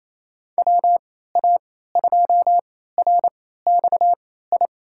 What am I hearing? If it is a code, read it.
Morse code, heard as a keyed tone, about 28 wpm, spaced as for 16 wpm, 700 Hz. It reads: WA2RXI
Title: USA Call Signs - 6 Characters - 3x Character Spacing - 28wpm